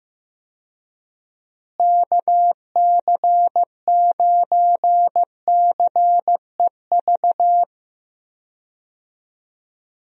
Morse code, heard as a keyed tone, about 15 wpm, 700 Hz. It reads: KC9CEV
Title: USA Call Signs - 6 Characters - Repeated 2x - 15wpm